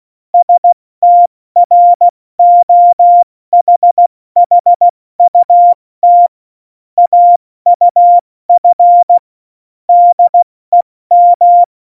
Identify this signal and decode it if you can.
Morse code, heard as a keyed tone, about 16 wpm, 700 Hz. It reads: STROHHUT AUF DEM
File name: Morsekurs / Witze / Witz 19. Juni 2018, 70 F80